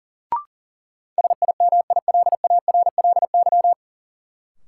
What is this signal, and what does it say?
Morse code, heard as a keyed tone, about 40 wpm, 700 Hz. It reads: SIMILARLY
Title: Sentences from Top 1400 Words - New Word Drill - 40wpm